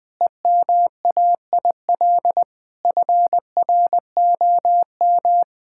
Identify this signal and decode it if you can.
Morse code, heard as a keyed tone, about 20 wpm, 700 Hz. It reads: EMAIL FROM
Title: Sentences with 7-Word Phrases - 20wpm